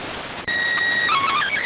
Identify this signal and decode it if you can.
SSTV (slow-scan television), PD 290.